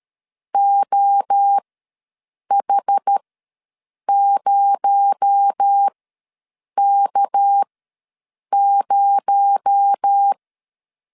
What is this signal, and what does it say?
Morse code, heard as a keyed tone, about 13 wpm, spaced as for 7 wpm, 800 Hz. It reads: OH0K0
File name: LEZIONI / 6 PKX0 / 6 PKX0 ws 12wpm 3448